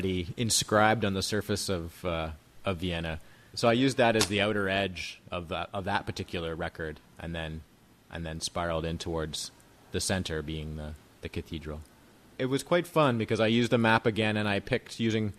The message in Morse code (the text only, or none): none